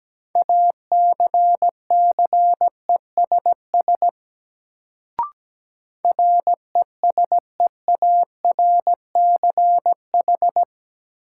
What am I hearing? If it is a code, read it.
Morse code, heard as a keyed tone, about 17 wpm, 700 Hz. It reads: ACCESS RESEARCH
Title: Sentences from Top 1000 Words - Review Part 4 - New Word Drill - 17wpm